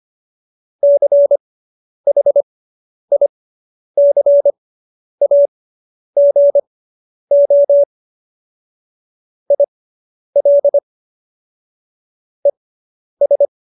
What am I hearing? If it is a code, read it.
Morse code, heard as a keyed tone, about 25 wpm, spaced as for 10 wpm, 570 Hz. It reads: CHICAGO IL ES